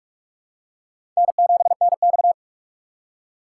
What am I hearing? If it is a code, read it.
Morse code, heard as a keyed tone, about 46 wpm, 700 Hz. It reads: N7NX
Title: USA Call Signs - 3x Character Spacing - 45wpm